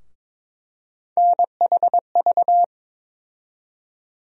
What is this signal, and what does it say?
Morse code, heard as a keyed tone, about 22 wpm, 700 Hz. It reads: NHV
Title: USA Call Sign Postfixes - 3 Letters 22wpm